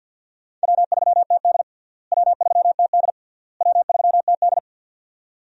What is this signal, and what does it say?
Morse code, heard as a keyed tone, about 48 wpm, 700 Hz. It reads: W3TD W3TD W3TD
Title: USA Call Signs - 4 Characters - Repeated 3x - 50wpm